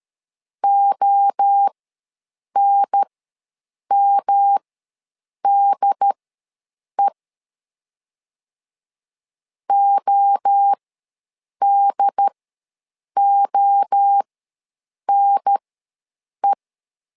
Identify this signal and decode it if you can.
Morse code, heard as a keyed tone, about 13 wpm, spaced as for 7 wpm, 800 Hz. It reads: ONMDE ODONE
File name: LEZIONI / 3 MWRO / 3 MWRO ws 12wpm 1914